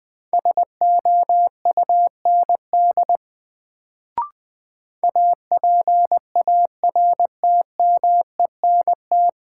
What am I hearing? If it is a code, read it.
Morse code, heard as a keyed tone, about 20 wpm, 700 Hz. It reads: SOUND APARTMENT
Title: Sentences from Top 900 Words - New Word Drill - 20wpm